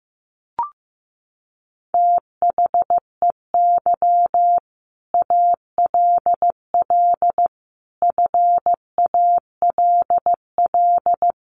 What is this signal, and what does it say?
Morse code, heard as a keyed tone, about 15 wpm, 700 Hz. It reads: THEY ALL FALL
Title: Sentences from Top 500 Words 15wpm